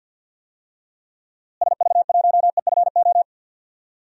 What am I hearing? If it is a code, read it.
Morse code, heard as a keyed tone, about 47 wpm, 700 Hz. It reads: SV1EFO
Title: DXCC-100 Call Signs - 6 Characters - Repeated 3x - 50wpm